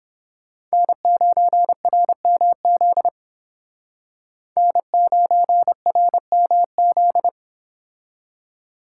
Morse code, heard as a keyed tone, about 29 wpm, 700 Hz. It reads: N9RMZ N9RMZ